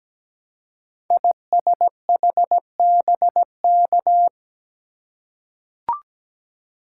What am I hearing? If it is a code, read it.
Morse code, heard as a keyed tone, about 17 wpm, 700 Hz. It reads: ISHBK
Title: Code Groups - 5 Letters - 17wpm